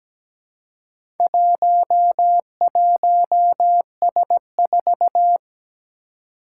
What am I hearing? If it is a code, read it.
Morse code, heard as a keyed tone, about 17 wpm, 700 Hz. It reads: 11S4
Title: Four Challenging Letter-Number Combinations 17wpm